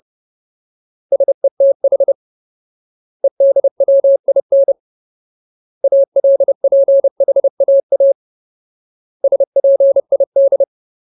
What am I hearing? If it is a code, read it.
Morse code, heard as a keyed tone, about 30 wpm, 550 Hz. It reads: SETH EDWIN ALPHAA SPID
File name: Ultimate Word List / 30WPM Ultimate